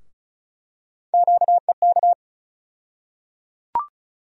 Morse code, heard as a keyed tone, about 34 wpm, 700 Hz. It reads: QEK
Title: USA Call Sign Postfixes - 3 Letters 35wpm